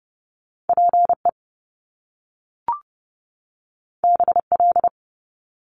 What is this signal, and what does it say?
Morse code, heard as a keyed tone, about 30 wpm, 700 Hz. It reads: PE BL
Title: Top Two Letter Combinations in Words 30wpm